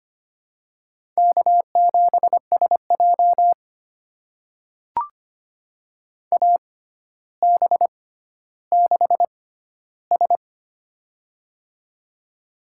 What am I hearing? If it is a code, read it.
Morse code, heard as a keyed tone, about 25 wpm, spaced as for 13 wpm, 700 Hz. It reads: K7SJ AB6S
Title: USA Call Signs - 4 Characters - 6x Character Spacing - 25wpm